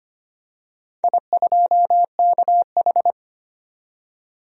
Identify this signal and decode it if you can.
Morse code, heard as a keyed tone, about 25 wpm, 700 Hz. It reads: I2KH
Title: DXCC-100 Call Signs - 4 Characters - 25wpm